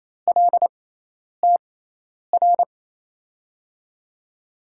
Morse code, heard as a keyed tone, about 28 wpm, spaced as for 10 wpm, 700 Hz. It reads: LTR
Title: USA Call Signs - 6 Characters - 6x Character Spacing - 28wpm